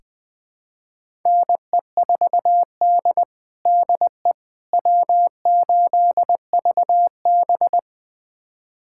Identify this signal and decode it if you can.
Morse code, heard as a keyed tone, about 20 wpm, 700 Hz. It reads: NE4D DE W8VB
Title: USA Callsign DE Callsign - 4 Characters - 20wpm